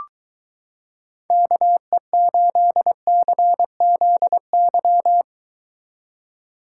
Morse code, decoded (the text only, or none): KE8CZY